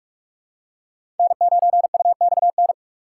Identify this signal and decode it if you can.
Morse code, heard as a keyed tone, about 47 wpm, 700 Hz. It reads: N9UXN
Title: USA Call Signs - 5 Characters - 3x Character Spacing - 45wpm